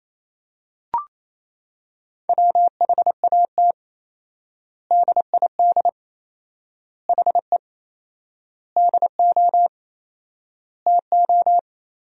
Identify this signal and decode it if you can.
Morse code, heard as a keyed tone, about 27 wpm, 700 Hz. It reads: WHAT DID HE DO TO
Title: 4x Word-Spacing - Sentences from Top 100 Words - 28wpm